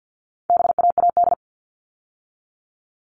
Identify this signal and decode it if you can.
Morse code, heard as a keyed tone, about 50 wpm, 700 Hz. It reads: BAAD
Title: All Cut Numbers - 4-Digit 50wpm